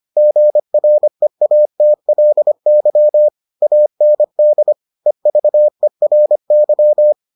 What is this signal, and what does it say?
Morse code, heard as a keyed tone, about 25 wpm, 600 Hz. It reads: GREATLY AND EVERY